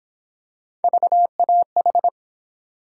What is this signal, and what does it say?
Morse code, heard as a keyed tone, about 30 wpm, 700 Hz. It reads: VAH